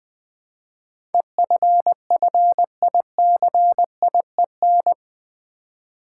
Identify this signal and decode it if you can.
Morse code, heard as a keyed tone, about 20 wpm, 700 Hz. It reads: EFFICIEN
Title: Build A Word - 4 Letters At A Time - 11-Letter Words - 20wpm